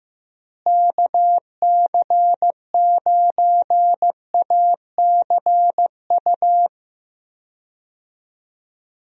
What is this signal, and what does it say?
Morse code, heard as a keyed tone, about 15 wpm, 700 Hz. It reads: KC9ACU
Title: USA Call Signs - 6 Characters - Repeated 3x - 15wpm